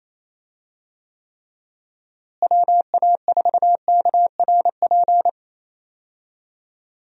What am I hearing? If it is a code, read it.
Morse code, heard as a keyed tone, about 28 wpm, 700 Hz. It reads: WA4KRP
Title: USA Call Signs - 6 Characters - Repeated 3x - 28wpm